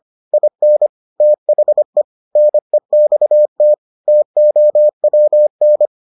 Morse code, heard as a keyed tone, about 25 wpm, 600 Hz. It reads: IN THE NEXT TOWN